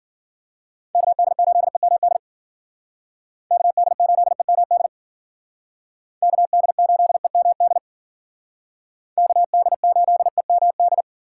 Morse code, heard as a keyed tone, about 49 wpm, 700 Hz. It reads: KD8EMD KD8EMD KD8EMD KD8EMD
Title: Speed Racing - USA Call Signs 40wpm